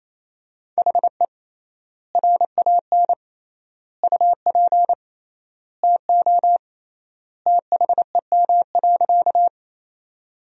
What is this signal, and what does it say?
Morse code, heard as a keyed tone, about 28 wpm, 700 Hz. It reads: HE RAN UP TO THEM.
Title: 3x Word-Spacing - Sentences from Top 500 Words - 28wpm